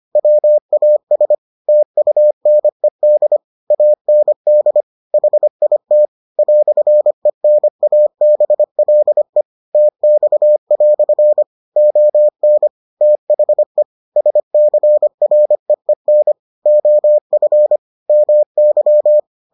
Morse code, heard as a keyed tone, about 25 wpm, 600 Hz. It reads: WAS TUNED AND HIT "ENABLE TX" ON THE SCREEN OF MY